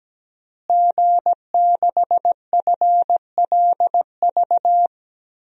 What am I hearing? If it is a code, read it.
Morse code, heard as a keyed tone, about 17 wpm, 700 Hz. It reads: G6FLV
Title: DXCC-100 Call Signs - 5 Characters - Repeated 3x - 17wpm